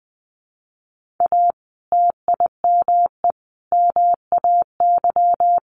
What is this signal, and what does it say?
Morse code, heard as a keyed tone, about 20 wpm, 700 Hz. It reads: A TIME MAY
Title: Sentences from Top 1500 Words 20wpm